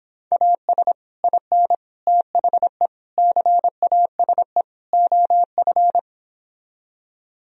Morse code, heard as a keyed tone, about 26 wpm, 700 Hz. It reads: AS IN THE CASE OF